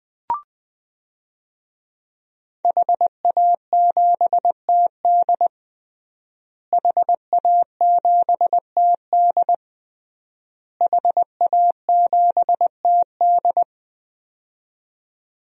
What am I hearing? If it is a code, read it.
Morse code, heard as a keyed tone, about 21 wpm, 700 Hz. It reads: HA7TD HA7TD HA7TD
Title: DXCC-100 Call Signs - 5 Characters - Repeated 3x - 20wpm